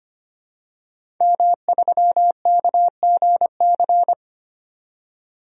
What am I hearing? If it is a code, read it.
Morse code, heard as a keyed tone, about 25 wpm, 700 Hz. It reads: M3KGC
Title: DXCC-100 Call Signs - 5 Characters - 25wpm